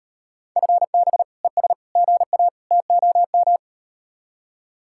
Morse code, heard as a keyed tone, about 38 wpm, 700 Hz. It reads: FB ES GA TOM